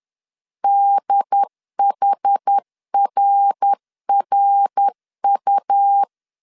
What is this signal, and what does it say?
Morse code, heard as a keyed tone, about 10 wpm, 800 Hz. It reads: DHRRU
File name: CW / DET / 1 3 1500C S C 10wpm 3335